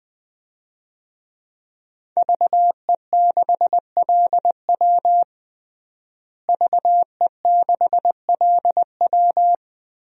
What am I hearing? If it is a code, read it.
Morse code, heard as a keyed tone, about 20 wpm, 700 Hz. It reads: VE6LW VE6LW
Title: DXCC-100 Call Signs - 5 Characters - Repeated 3x - 20wpm